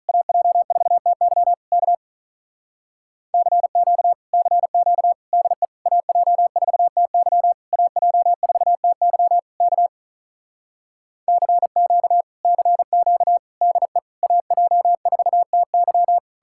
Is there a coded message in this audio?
Yes: Morse code from 0.1 to 16.2 s, heard as a keyed tone, about 47 wpm, 700 Hz. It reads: AJ4TY K CQ CQ DE AJ4TY AJ4TY K CQ CQ DE AJ4TY